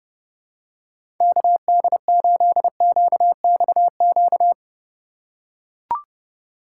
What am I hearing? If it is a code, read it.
Morse code, heard as a keyed tone, about 31 wpm, 700 Hz. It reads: KD8QXQ
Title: USA Call Signs - 6 Characters - 3x Character Spacing - 30wpm